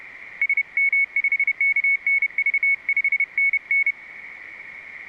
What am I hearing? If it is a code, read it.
Morse code, heard as a keyed tone, about 30 wpm, 2200 Hz. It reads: SM5KNV 5NN